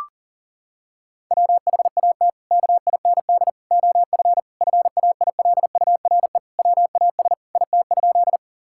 Morse code, heard as a keyed tone, about 40 wpm, 700 Hz. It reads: WHAT KIND OF FAILURE WAS IT?
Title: Sentences from Top 1000 Words - Review Part 6 - 40wpm